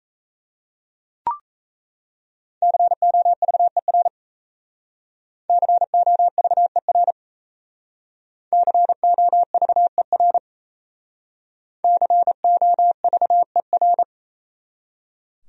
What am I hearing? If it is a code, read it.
Morse code, heard as a keyed tone, about 35 wpm, 700 Hz. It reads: COVER COVER COVER COVER
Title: Speed-Racing Top 300 Words 28wpm